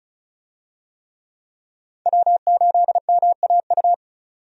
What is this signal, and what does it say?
Morse code, heard as a keyed tone, about 36 wpm, 700 Hz. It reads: W8MAU
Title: USA Call Signs - 5 Characters - 35wpm